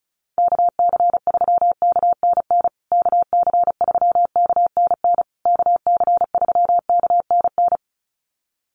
Morse code, heard as a keyed tone, about 34 wpm, 700 Hz. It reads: KC3KNN KC3KNN KC3KNN
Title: USA Call Signs Repeated 3x 35wpm